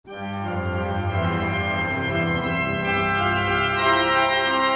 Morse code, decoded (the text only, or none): none